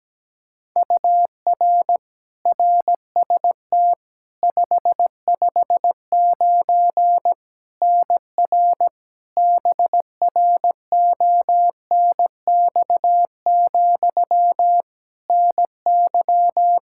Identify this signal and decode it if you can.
Morse code, heard as a keyed tone, about 17 wpm, 700 Hz. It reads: UR RST 559 NR BRONX, NY
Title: Practice QSO - Exchange 1 - Beginning - 17wpm